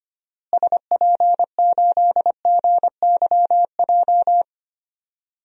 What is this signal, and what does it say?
Morse code, heard as a keyed tone, about 25 wpm, 700 Hz. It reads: SP8GYJ
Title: DXCC-100 Call Signs - 6 Characters - Repeated 2x - 25wpm